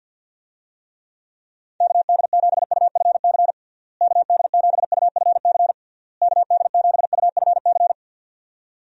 Morse code, heard as a keyed tone, about 50 wpm, 700 Hz. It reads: KD7UFC KD7UFC KD7UFC
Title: USA Call Signs - 6 Characters - Repeated 3x - 50wpm